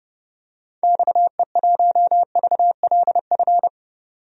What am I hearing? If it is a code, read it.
Morse code, heard as a keyed tone, about 31 wpm, 700 Hz. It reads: XE1VLF
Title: DXCC-100 Call Signs - 6 Characters - 30wpm